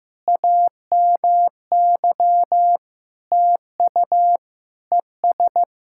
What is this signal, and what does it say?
Morse code, heard as a keyed tone, about 15 wpm, 700 Hz. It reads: AMY TU ES
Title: SOTA QSO Practice 15wpm